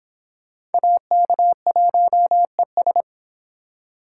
Morse code, heard as a keyed tone, about 30 wpm, 700 Hz. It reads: AK1ES